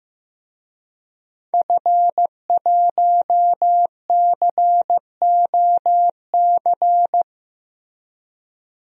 Morse code, heard as a keyed tone, about 15 wpm, 700 Hz. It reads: F1COC
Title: DXCC-100 Call Signs - 5 Characters - Repeated 2x - 15wpm